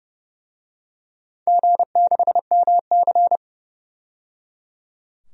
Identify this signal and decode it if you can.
Morse code, heard as a keyed tone, about 30 wpm, 700 Hz. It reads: G6MC
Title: DXCC-100 Call Signs - 4 Characters - 30wpm